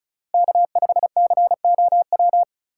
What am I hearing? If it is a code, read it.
Morse code, heard as a keyed tone, about 36 wpm, 700 Hz. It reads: K5COW
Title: USA Call Signs - 5 Characters - 2x Character Spacing - 35wpm